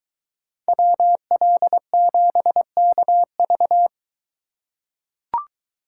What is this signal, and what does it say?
Morse code, heard as a keyed tone, about 21 wpm, 700 Hz. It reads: WL7KV